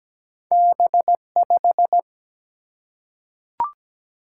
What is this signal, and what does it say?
Morse code, heard as a keyed tone, about 17 wpm, 700 Hz. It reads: B5
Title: Single-Letter Single-Number 17wpm